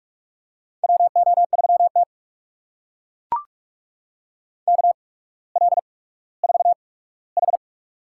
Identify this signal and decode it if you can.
Morse code, heard as a keyed tone, about 44 wpm, spaced as for 19 wpm, 690 Hz. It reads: WO3T KL4H
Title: USA Call Signs - 4 Characters - 8x Character Spacing - 45wpm